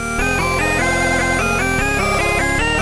There is no code.